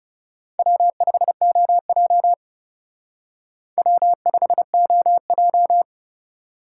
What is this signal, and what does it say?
Morse code, heard as a keyed tone, about 37 wpm, 700 Hz. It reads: W5OJ W5OJ
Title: Speed Racing - USA 1x2 Call Signs 30wpm